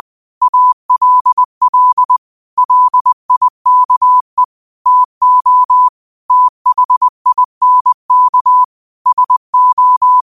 Morse code, heard as a keyed tone, about 20 wpm, 1000 Hz. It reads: ALL LIKE TO THINK SO
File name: qotdfiles / 20wpm